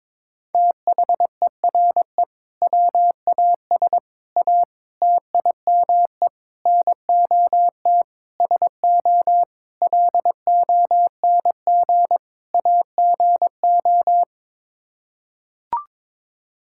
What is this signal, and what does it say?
Morse code, heard as a keyed tone, about 22 wpm, 700 Hz. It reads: THERE WAS A TIME NOT SO LONG AGO
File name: practice / 22wpm / Sets of 8 Words - 22wpm